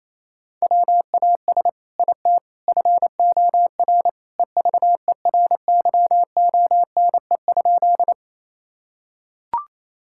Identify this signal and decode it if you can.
Morse code, heard as a keyed tone, about 27 wpm, 700 Hz. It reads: WAS IT FOR EVERYONE?
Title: Sentences from Top 1000 Words - Review Part 5 - 28wpm